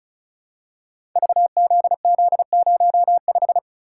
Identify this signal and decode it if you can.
Morse code, heard as a keyed tone, about 36 wpm, 700 Hz. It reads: VZZ05
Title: Code Groups - 5 Letter:Numbers - 35wpm